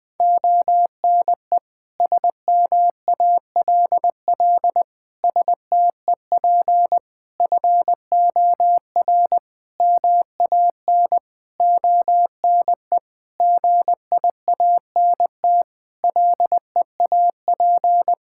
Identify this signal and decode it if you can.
Morse code, heard as a keyed tone, about 21 wpm, 700 Hz. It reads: ONE SMALL STEP FOR MAN ONE GIANT LEAP
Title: Sets of 9 Words - 20wpm